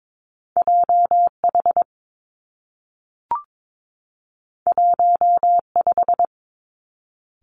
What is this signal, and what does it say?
Morse code, heard as a keyed tone, about 22 wpm, 700 Hz. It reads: J H 1 5
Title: Two Challenging Letter-Number Combinations 22wpm